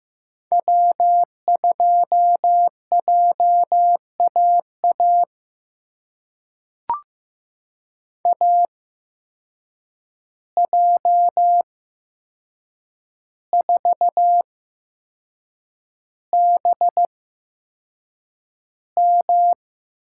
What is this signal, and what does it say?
Morse code, heard as a keyed tone, about 15 wpm, spaced as for 6 wpm, 700 Hz. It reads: W2JAA AJ4BM
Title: USA Call Signs - 5 Characters - 8x Character Spacing - 15wpm